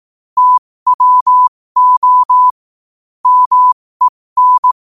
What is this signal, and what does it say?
Morse code, heard as a keyed tone, about 18 wpm, spaced as for 15 wpm, 1000 Hz. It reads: TWO MEN
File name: qotdfiles / 15wpm